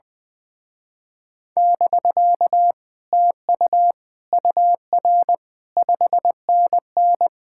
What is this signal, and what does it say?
Morse code, heard as a keyed tone, about 20 wpm, 700 Hz. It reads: <BK> TU UR 5NN
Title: Speed-Racing Common QSO Phrases Encore - 20wpm